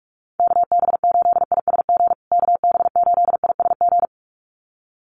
Morse code, heard as a keyed tone, about 44 wpm, 700 Hz. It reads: KB8ISG KB8ISG
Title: USA Call Signs Repeated 2x 45wpm